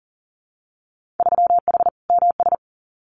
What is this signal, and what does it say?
Morse code, heard as a keyed tone, about 42 wpm, 700 Hz. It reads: 3H MS
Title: Winter Field Day Exchanges Encore 40wpm